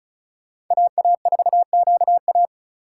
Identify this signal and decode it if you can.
Morse code, heard as a keyed tone, about 36 wpm, 700 Hz. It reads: AA4QA
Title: USA Call Signs - 5 Characters - 3x Character Spacing - 35wpm